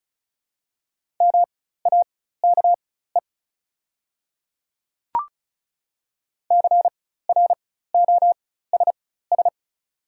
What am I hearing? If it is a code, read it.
Morse code, heard as a keyed tone, about 36 wpm, spaced as for 16 wpm, 690 Hz. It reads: MAKE CROSS
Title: 4x Character Spacing - Top 300 Words - 35wpm